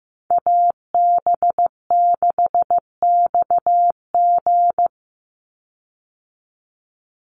Morse code, heard as a keyed tone, about 15 wpm, 700 Hz. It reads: AB6XG
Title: Call Signs Repeated 2x - 15wpm